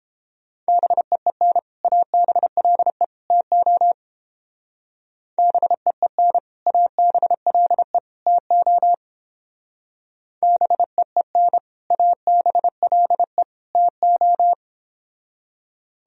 Morse code, heard as a keyed tone, about 29 wpm, 700 Hz. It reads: BEEN ABLE TO BEEN ABLE TO BEEN ABLE TO